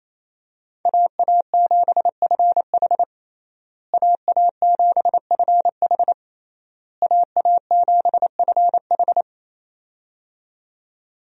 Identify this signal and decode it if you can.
Morse code, heard as a keyed tone, about 28 wpm, 700 Hz. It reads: AA7FH AA7FH AA7FH
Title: USA Call Signs - 5 Characters - Repeated 3x - 28wpm